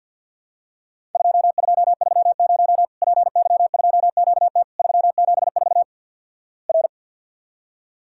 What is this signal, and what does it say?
Morse code, heard as a keyed tone, about 50 wpm, 700 Hz. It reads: 2230 PY2QT 374 R
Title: Worked All Europe DX Contest - QTC Exchange - 50wpm